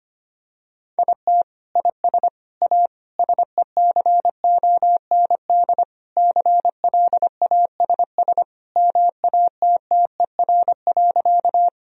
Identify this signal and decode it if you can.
Morse code, heard as a keyed tone, about 25 wpm, 700 Hz. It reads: IT IS A SECOND CLASS MATTER.
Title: Sentences from Top 1000 Words - Review Part 2 - 25wpm